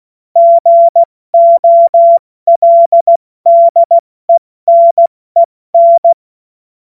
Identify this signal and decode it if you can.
Morse code, heard as a keyed tone, about 16 wpm, 680 Hz. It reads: GOLDENEN
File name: Morsekurs / Witze / Witz 14. März 2023 70 F80